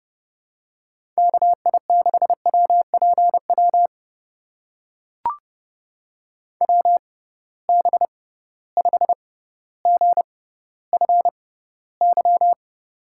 Morse code, heard as a keyed tone, about 30 wpm, spaced as for 15 wpm, 700 Hz. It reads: KI6WPW WB5GFY
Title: USA Call Signs - 6 Characters - 6x Character Spacing - 30wpm